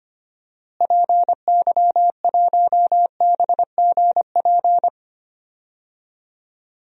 Morse code, heard as a keyed tone, about 25 wpm, 700 Hz. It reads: PY1BGP
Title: DXCC-100 Call Signs - 6 Characters - Repeated 3x - 25wpm